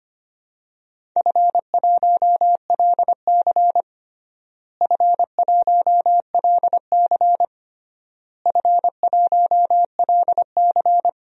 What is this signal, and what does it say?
Morse code, heard as a keyed tone, about 25 wpm, 700 Hz. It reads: F1LC F1LC F1LC
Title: DXCC-100 Call Signs - 4 Characters - Repeated 3x - 25wpm